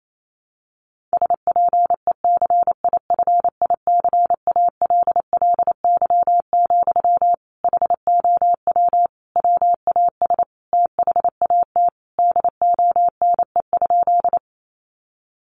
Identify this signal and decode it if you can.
Morse code, heard as a keyed tone, about 28 wpm, 700 Hz. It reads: SPECIFICALLY, HOW WAS THAT DONE?
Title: Sentences from Top 1100 Words 28wpm